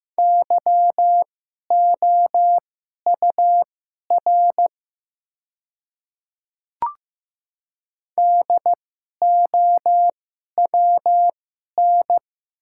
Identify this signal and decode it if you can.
Morse code, heard as a keyed tone, about 15 wpm, spaced as for 11 wpm, 690 Hz. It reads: YOUR DOWN
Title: 2x Character Spacing - Top 300 Words - 15wpm